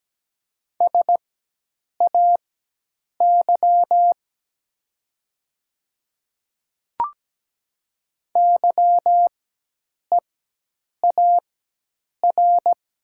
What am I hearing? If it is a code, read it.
Morse code, heard as a keyed tone, about 17 wpm, spaced as for 8 wpm, 700 Hz. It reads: SAY YEAR
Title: 4x Character Spacing - Top 200 Words - 17wpm